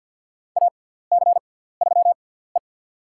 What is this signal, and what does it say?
Morse code, heard as a keyed tone, about 47 wpm, spaced as for 17 wpm, 700 Hz. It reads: AC3E
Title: USA Call Signs - 4 Characters - 6x Character Spacing - 50wpm